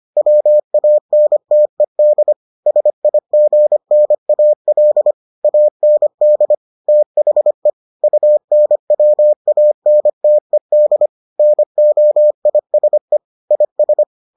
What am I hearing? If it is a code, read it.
Morse code, heard as a keyed tone, about 25 wpm, 600 Hz. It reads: WANTED SIGNAL AND THE UNWANTED NOISE IS